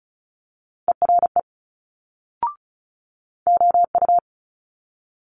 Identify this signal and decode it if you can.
Morse code, heard as a keyed tone, about 36 wpm, 700 Hz. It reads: ERE OU
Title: Multi-letter Phonemes Quick Fire 35wpm